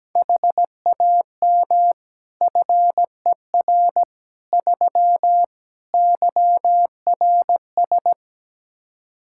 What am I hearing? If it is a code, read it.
Morse code, heard as a keyed tone, about 17 wpm, 700 Hz. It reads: HAM FER 3 YRS
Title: Practice QSO - Exchange 3 - More Info - 17wpm